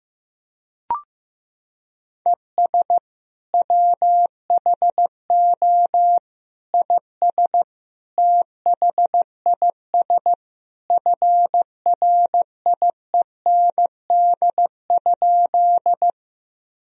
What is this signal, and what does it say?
Morse code, heard as a keyed tone, about 15 wpm, 700 Hz. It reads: ES WHO IS THIS FRIEND?
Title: Sentences from Top 300 Words and Abbreviations 15wpm